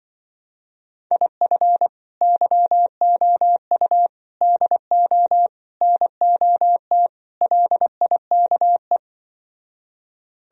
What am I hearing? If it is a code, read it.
Morse code, heard as a keyed tone, about 24 wpm, 700 Hz. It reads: IF YOU DO NOT LIKE
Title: Speed-Racing - Sets of 5 Words 20wpm